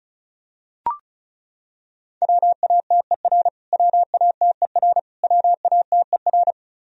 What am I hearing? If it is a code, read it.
Morse code, heard as a keyed tone, about 34 wpm, 700 Hz. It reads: WATER WATER WATER
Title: Top 100 Words Repeated 35wpm